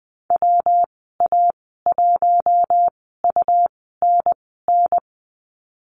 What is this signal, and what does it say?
Morse code, heard as a keyed tone, about 20 wpm, spaced as for 15 wpm, 700 Hz. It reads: WA1UNN
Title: 140 Lesson 23g - Letter f - Call Signs 20wpm